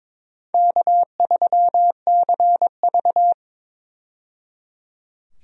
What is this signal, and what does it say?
Morse code, heard as a keyed tone, about 21 wpm, 700 Hz. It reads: K3CV
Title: USA Call Signs - 4 Characters - 22wpm